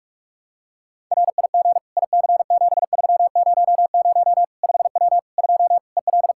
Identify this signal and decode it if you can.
Morse code, heard as a keyed tone, about 46 wpm, 700 Hz. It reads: RIG IC7300 5W 2 EL
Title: K1USN SST QSO Practice 45wpm